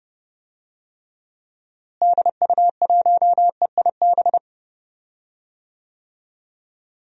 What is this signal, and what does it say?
Morse code, heard as a keyed tone, about 30 wpm, 700 Hz. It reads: DU1EIB
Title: DXCC-100 Call Signs - 6 Characters - Repeated 3x - 30wpm